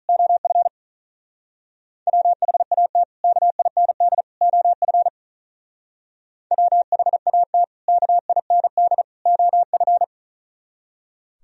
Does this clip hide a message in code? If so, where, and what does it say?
Yes: Morse code from 0.1 to 10.0 s, heard as a keyed tone, about 46 wpm, 700 Hz. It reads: OF WHAT KIND OF WHAT KIND OF